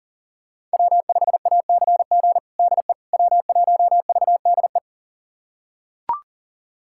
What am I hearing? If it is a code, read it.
Morse code, heard as a keyed tone, about 40 wpm, 700 Hz. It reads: W5ACG DE W1VDE
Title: USA Callsign DE Callsign - 5 Characters - 40wpm